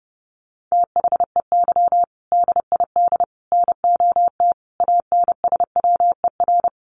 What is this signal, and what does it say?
Morse code, heard as a keyed tone, about 30 wpm, 700 Hz. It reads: THEY DID NOT ANSWER
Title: Sentences from Top 600 Words 30wpm